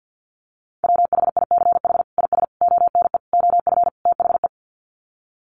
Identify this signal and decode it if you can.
Morse code, heard as a keyed tone, about 50 wpm, 700 Hz. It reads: WHICH IS ONE OF THE
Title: Sets of 5 Words Encore 50wpm